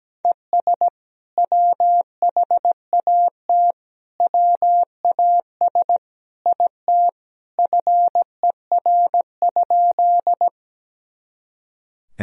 Morse code, heard as a keyed tone, about 17 wpm, 700 Hz. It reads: ES WHAT WAS IT FER?
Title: Sentences from Top 100 Words and Abbreviations 17wpm